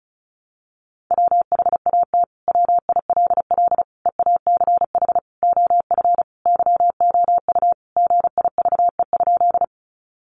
Morse code, heard as a keyed tone, about 34 wpm, 700 Hz. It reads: WHAT WILL EACH OF YOU GIVE?
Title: Sentences from Top 200 Words 35wpm